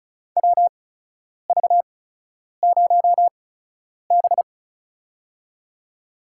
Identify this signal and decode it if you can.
Morse code, heard as a keyed tone, about 36 wpm, spaced as for 10 wpm, 700 Hz. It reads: WV0B
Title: USA Call Signs - 4 Characters - 8x Character Spacing - 35wpm